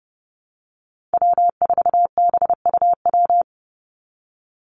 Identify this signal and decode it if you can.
Morse code, heard as a keyed tone, about 28 wpm, 700 Hz. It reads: W4BUW
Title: Call Signs Repeated 2x - 30wpm